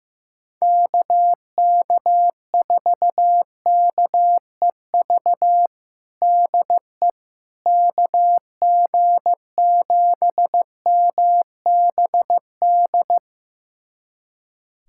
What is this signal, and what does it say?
Morse code, heard as a keyed tone, about 15 wpm, 700 Hz. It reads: KK4KEV DE KG7MBD
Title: USA Callsign DE Callsign - 6 Characters - 15wpm